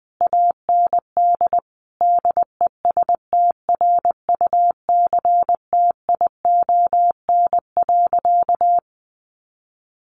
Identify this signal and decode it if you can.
Morse code, heard as a keyed tone, about 19 wpm, 700 Hz. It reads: AND DESTRUCTION.
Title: A Sound Of Thunder 20wpm